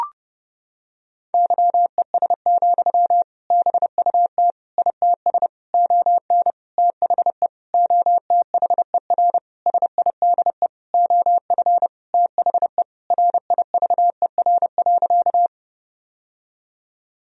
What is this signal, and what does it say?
Morse code, heard as a keyed tone, about 32 wpm, 700 Hz. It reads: YES, BUT ITS ON THE OTHER SIDE OF THE RIVER.
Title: Sentences from Top 1000 Words - Review Part 1 - 30wpm